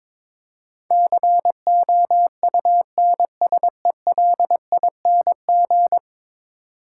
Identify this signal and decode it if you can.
Morse code, heard as a keyed tone, about 22 wpm, 700 Hz. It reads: COUNSELING
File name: practice / 22wpm / 10 Letter Words - First 3 Letters Given - 22wpm